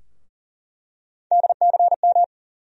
Morse code, heard as a keyed tone, about 40 wpm, 700 Hz. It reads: DCM
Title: USA Call Sign Postfixes - 3 Letters 40wpm